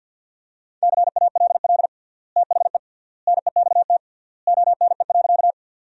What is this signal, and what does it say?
Morse code, heard as a keyed tone, about 51 wpm, 700 Hz. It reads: CALL THE NEXT ONE.
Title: 3x Word-Spacing - Sentences from Top 300 Words - 50wpm